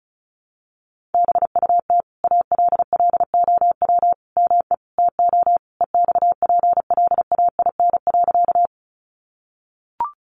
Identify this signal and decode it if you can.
Morse code, heard as a keyed tone, about 34 wpm, 700 Hz. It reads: BUT ALLOW ME TO EXPLAIN.
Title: Sentences from Top 1100 Words 35wpm